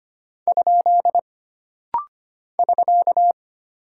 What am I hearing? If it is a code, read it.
Morse code, heard as a keyed tone, about 25 wpm, 700 Hz. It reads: ? <SK>
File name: Single Punctuation-Prosign - Rapid-Fire - 25wpm